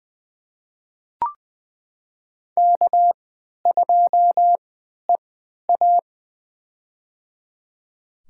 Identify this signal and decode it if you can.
Morse code, heard as a keyed tone, about 20 wpm, spaced as for 11 wpm, 700 Hz. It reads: K2EA
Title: USA Call Signs - 4 Characters - 3x Character Spacing - 20wpm